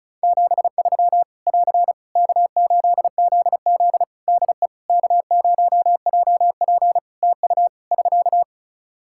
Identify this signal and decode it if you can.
Morse code, heard as a keyed tone, about 36 wpm, 700 Hz. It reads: 73 <AR> K8ZZ DE K0JP TU <SK>
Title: Practice QSO - Exchange 4 - Ending - 35wpm